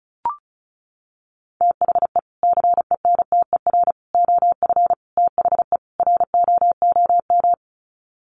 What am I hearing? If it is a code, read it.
Morse code, heard as a keyed tone, about 34 wpm, 700 Hz. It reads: THE CENTER OF THE ROOM
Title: Sets of 5 Words Encore 35wpm